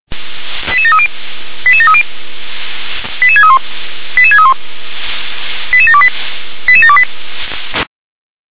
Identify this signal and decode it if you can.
Five-tone selective call (ZVEI), heard as quick runs of five tones, tones of about 70 ms. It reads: 88730 88730 88631 88631 88728 88728
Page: SDR/ZVEI